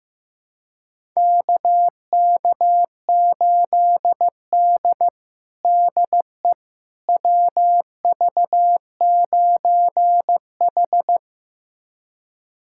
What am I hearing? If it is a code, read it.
Morse code, heard as a keyed tone, about 15 wpm, 700 Hz. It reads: KK8D DE WV9H
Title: USA Callsign DE Callsign - 4 Characters - 15wpm